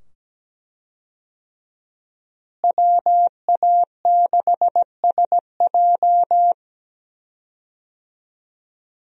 Morse code, heard as a keyed tone, about 17 wpm, 700 Hz. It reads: WA6SJ
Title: USA Call Signs - 5 Characters - Repeated 2x - 17wpm